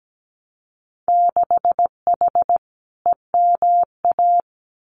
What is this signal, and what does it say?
Morse code, heard as a keyed tone, about 17 wpm, 700 Hz. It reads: 6H EMA
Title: Winter Field Day Exchanges Encore 17wpm